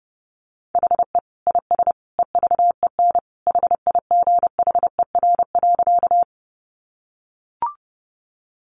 Morse code, heard as a keyed tone, about 30 wpm, 700 Hz. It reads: HE IS EVEN HIGHER.
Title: Additional Easy Phrases 30wpm